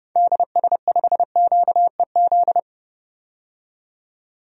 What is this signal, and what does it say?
Morse code, heard as a keyed tone, about 30 wpm, 700 Hz. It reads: DS5QEZ
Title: DXCC-100 Call Signs - 6 Characters - 30wpm